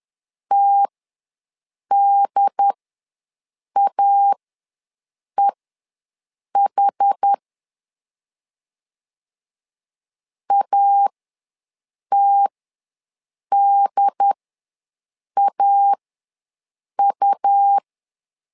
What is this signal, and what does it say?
Morse code, heard as a keyed tone, about 10 wpm, spaced as for 6 wpm, 800 Hz. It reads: TDAEH ATDAU
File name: LEZIONI / 2 UHDN / 2 UHD ws 10wpm 2058